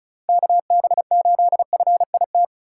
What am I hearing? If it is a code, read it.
Morse code, heard as a keyed tone, about 36 wpm, 700 Hz. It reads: KB8FIT
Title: USA Call Signs - 6 Characters - 4x Character Spacing - 35wpm